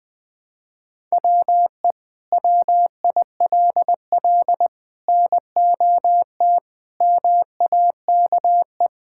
Morse code, heard as a keyed tone, about 20 wpm, 700 Hz. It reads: WE WILL NOT MAKE
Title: Sentences from Top 1000 Words - Review Part 5 - 20wpm